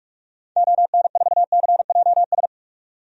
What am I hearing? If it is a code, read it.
Morse code, heard as a keyed tone, about 44 wpm, 700 Hz. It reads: ON4CJS
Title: DXCC-100 Call Signs - 6 Characters - 45wpm